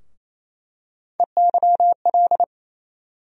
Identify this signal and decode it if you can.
Morse code, heard as a keyed tone, about 27 wpm, 700 Hz. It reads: EYL
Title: USA Call Sign Postfixes - 3 Letters 28wpm